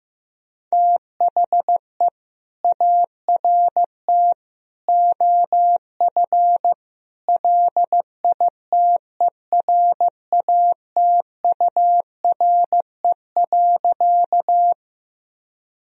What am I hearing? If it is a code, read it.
Morse code, heard as a keyed tone, about 15 wpm, 700 Hz. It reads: THE ART OF LITERATURE.
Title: Sentences from Top 1000 Words - Review Part 3 - 15wpm